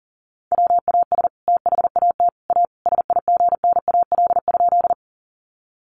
Morse code, heard as a keyed tone, about 40 wpm, 700 Hz. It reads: WAS THAT A SIGNAL?
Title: Sentences from Top 1400 Words 40wpm